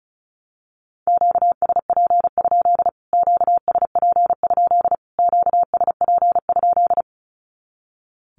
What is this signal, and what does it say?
Morse code, heard as a keyed tone, about 34 wpm, 700 Hz. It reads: QSP? QSP? QSP?
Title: Q-Codes Repeated 3x 35wpm